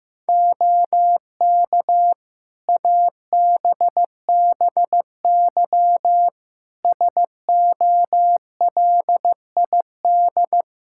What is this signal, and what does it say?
Morse code, heard as a keyed tone, about 15 wpm, 700 Hz. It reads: OK ABBY SOLID
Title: Practice QSO - Exchange 3 - More Info - 15wpm